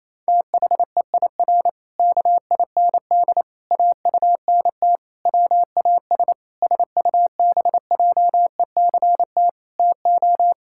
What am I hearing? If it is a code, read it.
Morse code, heard as a keyed tone, about 28 wpm, 700 Hz. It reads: THEIR KIND AUNT WAS SUBJECT TO
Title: Pangram Challenge v2 - 28wpm